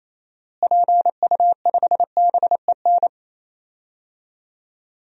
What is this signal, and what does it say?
Morse code, heard as a keyed tone, about 28 wpm, 700 Hz. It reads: PU5BEN
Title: DXCC-100 Call Signs - 6 Characters - Repeated 3x - 28wpm